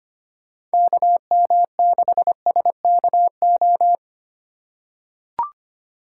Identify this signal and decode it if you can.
Morse code, heard as a keyed tone, about 25 wpm, 700 Hz. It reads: KM6SKO
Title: USA Call Signs - 6 Characters - 4x Character Spacing - 25wpm